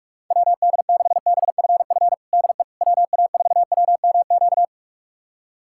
Morse code, heard as a keyed tone, about 46 wpm, 700 Hz. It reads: WD6BFF DE WA4WMQ
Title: USA Callsign DE Callsign - 6 Characters - 45wpm